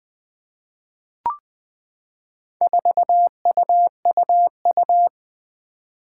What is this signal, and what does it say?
Morse code, heard as a keyed tone, about 20 wpm, 700 Hz. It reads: 4UUU
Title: Four Challenging Letter-Number Combinations 20wpm